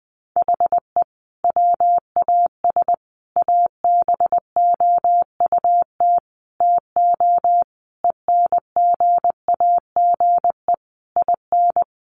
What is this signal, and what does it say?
Morse code, heard as a keyed tone, about 20 wpm, 700 Hz. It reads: HE WAS ABOUT TO ENGAGE IN
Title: Sentences from Top 1500 Words 20wpm